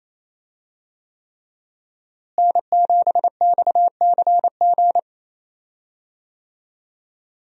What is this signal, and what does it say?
Morse code, heard as a keyed tone, about 28 wpm, 700 Hz. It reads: N7XCG
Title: USA Call Signs - 5 Characters - Repeated 3x - 28wpm